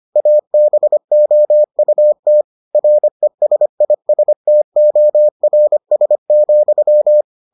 Morse code, heard as a keyed tone, about 25 wpm, 600 Hz. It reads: ABOUT RESISTORS,